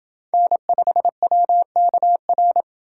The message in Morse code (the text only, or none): N5WKR